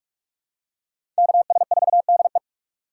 Morse code, heard as a keyed tone, about 46 wpm, 700 Hz. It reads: KS4DE
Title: USA Call Signs - 5 Characters - 3x Character Spacing - 45wpm